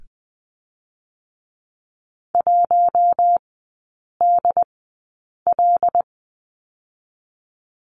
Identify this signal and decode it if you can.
Morse code, heard as a keyed tone, about 20 wpm, spaced as for 8 wpm, 700 Hz. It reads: 1DL
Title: 107 Lesson 18e - Letter m - Three Characters 20wpm